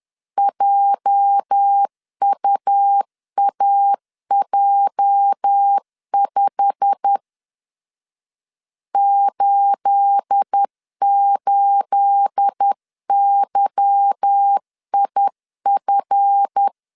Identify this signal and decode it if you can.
Morse code, heard as a keyed tone, about 10 wpm, 800 Hz. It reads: JUAJ5 88YIF
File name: LEZIONI / 9 28 10wpm 2842